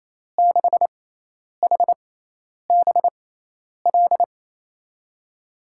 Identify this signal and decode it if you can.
Morse code, heard as a keyed tone, about 28 wpm, spaced as for 10 wpm, 700 Hz. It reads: 6HBL
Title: USA Call Signs - 6 Characters - 6x Character Spacing - 28wpm